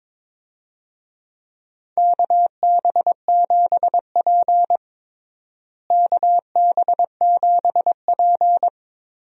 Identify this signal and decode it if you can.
Morse code, heard as a keyed tone, about 22 wpm, 700 Hz. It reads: KB7P KB7P
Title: USA Call Signs - 4 Characters - Repeated 3x - 22wpm